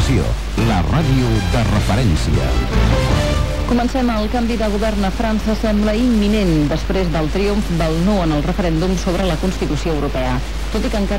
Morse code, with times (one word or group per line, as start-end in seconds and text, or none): none